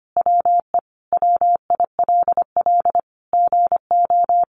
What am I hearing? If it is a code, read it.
Morse code, heard as a keyed tone, about 25 wpm, 700 Hz. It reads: WE WILL GO
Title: Sentences from Top 1300 Words 25wpm